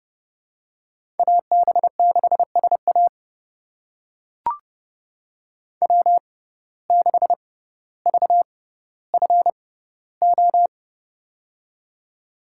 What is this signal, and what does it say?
Morse code, heard as a keyed tone, about 30 wpm, spaced as for 10 wpm, 700 Hz. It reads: AB6S AW6VFO
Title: USA Call Signs - 6x Character Spacing - 30wpm